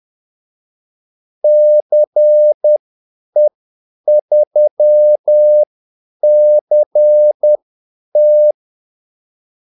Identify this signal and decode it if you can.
Morse code, heard as a keyed tone, about 10 wpm, spaced as for 8 wpm, 600 Hz. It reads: CE3CT
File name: Callsigns-10-8wpm